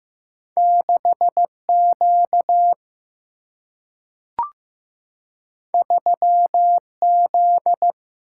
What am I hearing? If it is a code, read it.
Morse code, heard as a keyed tone, about 15 wpm, 700 Hz. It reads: 6 Q 3 Z
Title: Single-Letter Single-Number 15wpm